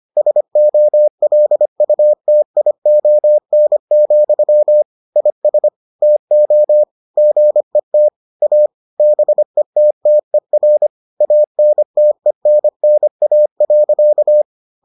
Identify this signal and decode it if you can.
Morse code, heard as a keyed tone, about 25 wpm, 600 Hz. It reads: SOLUTION, IS TO GET A BETTER ANTENNA.